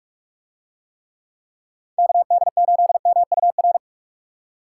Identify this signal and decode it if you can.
Morse code, heard as a keyed tone, about 45 wpm, 700 Hz. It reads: KD8MUR